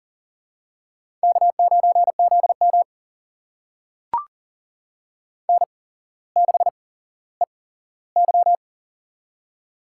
Morse code, heard as a keyed tone, about 40 wpm, spaced as for 17 wpm, 700 Hz. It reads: K9ZM N6EY
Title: USA Call Signs - 4 Characters - 8x Character Spacing - 40wpm